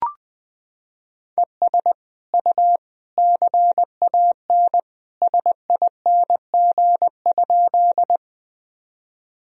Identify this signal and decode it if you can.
Morse code, heard as a keyed tone, about 21 wpm, 700 Hz. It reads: ES U CAN SING?
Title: Sentences from Top 400 Words and Abbreviations 20wpm